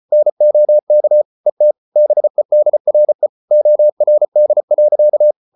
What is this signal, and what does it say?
Morse code, heard as a keyed tone, about 34 wpm, 600 Hz. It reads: NOK ET BEDRE ORD.
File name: Ferie QRQ 34wpm